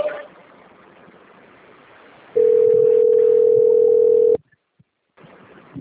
Ringback tone (North America), 440 + 480 Hz.